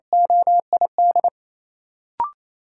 Morse code, heard as a keyed tone, about 28 wpm, 700 Hz. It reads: OID
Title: Sentences from Top 1000 Words - Review Part 6 - New Word Drill - 28wpm